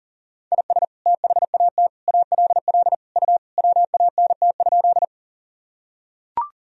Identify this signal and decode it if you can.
Morse code, heard as a keyed tone, about 40 wpm, 700 Hz. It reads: IS THAT ALL U WANT?
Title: Sentences from Top 200 Words and Abbreviations 40wpm